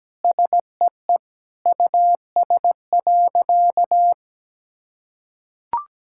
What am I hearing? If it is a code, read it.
Morse code, heard as a keyed tone, about 17 wpm, 700 Hz. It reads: SEE US.
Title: Sentences from Top 1600 Words 17wpm